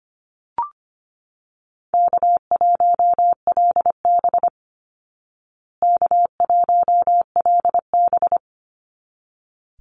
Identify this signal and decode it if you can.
Morse code, heard as a keyed tone, about 25 wpm, 700 Hz. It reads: K1LB K1LB
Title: Easy Call Signs Repeated 2x - 25wpm